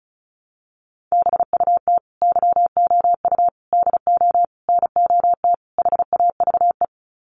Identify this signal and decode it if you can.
Morse code, heard as a keyed tone, about 34 wpm, 700 Hz. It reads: BUT YOU DO NOT HAVE
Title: Sets of 5 Words Encore 35wpm